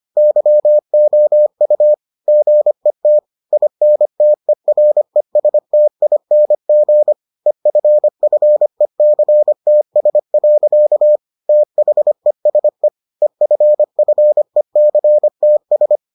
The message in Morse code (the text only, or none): YOU GET INTERESTING EFFECTS. THESE EFFECTS